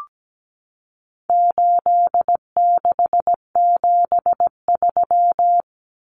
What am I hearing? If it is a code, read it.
Morse code, heard as a keyed tone, about 17 wpm, 700 Hz. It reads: 8673
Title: Common Cut Numbers - 4-Digit 17wpm